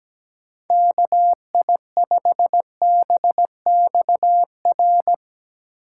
Morse code, heard as a keyed tone, about 17 wpm, 700 Hz. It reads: KI5BXR
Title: Speed Racing - USA 2x3 Call Signs 17wpm